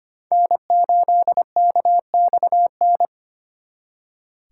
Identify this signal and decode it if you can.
Morse code, heard as a keyed tone, about 31 wpm, 700 Hz. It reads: N8KXN
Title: Speed Racing - USA 1x3 Call Signs 25wpm